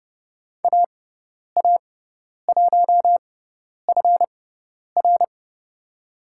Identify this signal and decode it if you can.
Morse code, heard as a keyed tone, about 31 wpm, spaced as for 10 wpm, 700 Hz. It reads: AA1FR
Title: USA Call Signs - 5 Characters - 6x Character Spacing - 30wpm